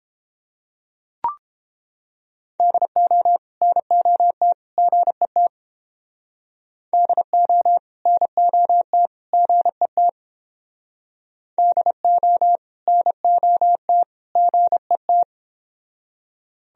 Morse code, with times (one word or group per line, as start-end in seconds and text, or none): 2.6-3.4 DO
3.6-4.5 NOT
4.8-5.5 GET
6.9-7.8 DO
8.1-9.1 NOT
9.3-10.1 GET
11.6-12.6 DO
12.9-14.0 NOT
14.4-15.2 GET